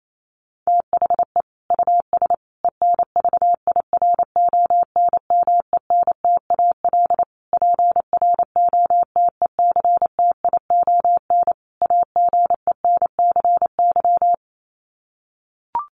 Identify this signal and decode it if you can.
Morse code, heard as a keyed tone, about 28 wpm, 700 Hz. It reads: THE US ENVIRONMENTAL PROTECTION AGENCY
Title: Sets of 5 Words Encore2 28wpm